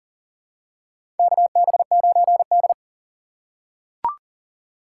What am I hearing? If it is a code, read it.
Morse code, heard as a keyed tone, about 37 wpm, 700 Hz. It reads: KB9D
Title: USA Call Signs - 4 Characters - 4x Character Spacing - 40wpm